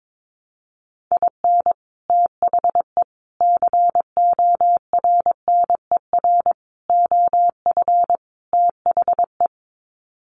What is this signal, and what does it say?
Morse code, heard as a keyed tone, about 22 wpm, 700 Hz. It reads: IN THE CORNER OF THE
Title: Sets of 5 Words Encore 22wpm